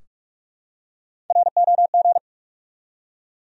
Morse code, heard as a keyed tone, about 46 wpm, 700 Hz. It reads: ROG
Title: USA Common Names - 45wpm